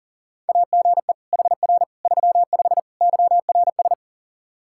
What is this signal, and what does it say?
Morse code, heard as a keyed tone, about 40 wpm, 700 Hz. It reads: AGE HR 35 YRS
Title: Practice QSO - Exchange 3 - More Info - 40wpm